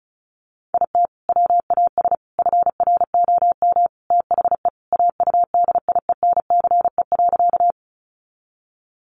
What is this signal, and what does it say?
Morse code, heard as a keyed tone, about 34 wpm, 700 Hz. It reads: IT WAS FROM THE AUDIENCE.
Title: Sentences from Top 800 Words 35wpm